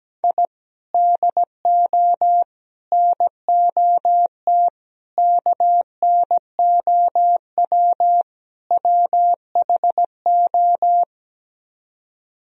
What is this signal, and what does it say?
Morse code, heard as a keyed tone, about 17 wpm, 700 Hz. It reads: I DO NOT KNOW WHO
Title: Speed-Racing - Sets of 5 Words 17wpm